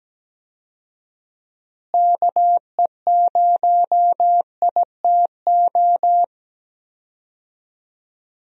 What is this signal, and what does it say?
Morse code, heard as a keyed tone, about 17 wpm, 700 Hz. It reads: KE0ITO
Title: USA Call Signs - 6 Characters - Repeated 2x - 17wpm